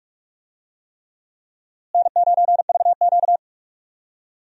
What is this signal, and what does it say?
Morse code, heard as a keyed tone, about 44 wpm, 700 Hz. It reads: N9VQ
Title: USA Call Signs - 4 Characters - Repeated 3x - 45wpm